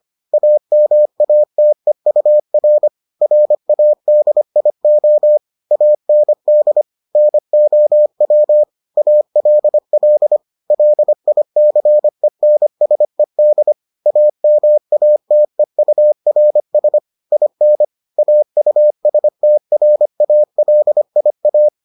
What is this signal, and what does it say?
Morse code, heard as a keyed tone, about 25 wpm, 600 Hz. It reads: AMATEUR RADIO AND NOW ALL LICENSED AMATEURS IN AUSTRALIA